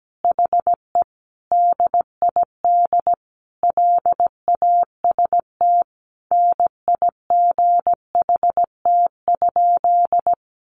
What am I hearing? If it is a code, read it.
Morse code, heard as a keyed tone, about 17 wpm, 700 Hz. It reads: HE DID LAST NIGHT?
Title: Sentences from Top 300 Words 17wpm